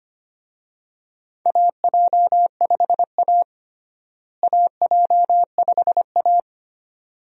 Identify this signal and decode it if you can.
Morse code, heard as a keyed tone, about 25 wpm, 700 Hz. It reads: AJ5A AJ5A
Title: USA Call Signs - 4 Characters - Repeated 2x - 25wpm